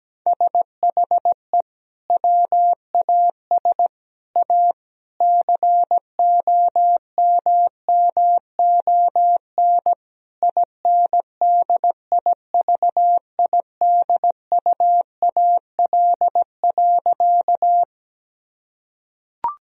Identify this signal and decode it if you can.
Morse code, heard as a keyed tone, about 17 wpm, 700 Hz. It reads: SHE WAS A COMMON INDIVIDUAL.
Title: Sentences from Top 1000 Words - Review Part 4 - 17wpm